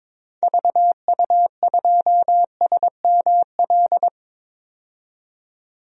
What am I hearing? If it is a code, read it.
Morse code, heard as a keyed tone, about 22 wpm, 700 Hz. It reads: VU2SML
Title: DXCC-100 Call Signs - 6 Characters - 22wpm